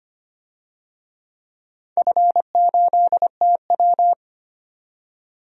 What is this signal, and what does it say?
Morse code, heard as a keyed tone, about 25 wpm, 700 Hz. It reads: F8TW
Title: DXCC-100 Call Signs - 4 Characters - 25wpm